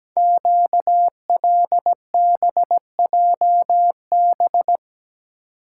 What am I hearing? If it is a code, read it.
Morse code, heard as a keyed tone, about 17 wpm, 700 Hz. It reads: QLBJB
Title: Code Groups - 5 Letters - 17wpm